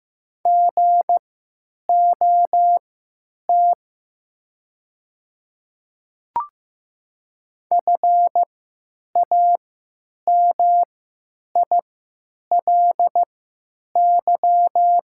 Morse code, heard as a keyed tone, about 15 wpm, spaced as for 9 wpm, 700 Hz. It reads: GOT FAMILY
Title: 3x Character Spacing - Top 500 Words - 15wpm